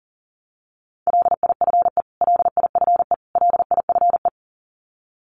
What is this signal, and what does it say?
Morse code, heard as a keyed tone, about 40 wpm, 700 Hz. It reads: LIFE LIFE LIFE
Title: Top 300 Words Repeated 40wpm